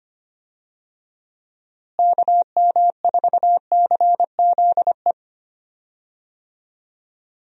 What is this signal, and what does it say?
Morse code, heard as a keyed tone, about 25 wpm, 700 Hz. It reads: KM4CZE
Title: USA Call Signs - 6 Characters - Repeated 3x - 25wpm